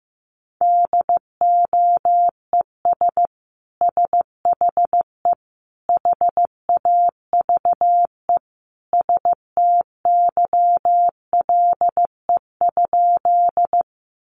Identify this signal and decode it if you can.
Morse code, heard as a keyed tone, about 15 wpm, 700 Hz. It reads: DOES SHE HAVE STYLE?
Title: Sentences from Top 800 Words 15wpm